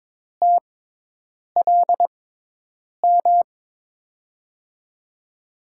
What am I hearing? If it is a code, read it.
Morse code, heard as a keyed tone, about 21 wpm, spaced as for 8 wpm, 700 Hz. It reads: TLM
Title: USA Call Signs - 5 Characters - 6x Character Spacing - 22wpm